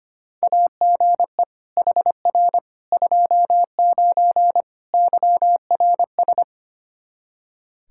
Morse code, heard as a keyed tone, about 25 wpm, 700 Hz. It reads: AGE HR 29 YRS
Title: Practice QSO - Exchange 3 - More Info - 25wpm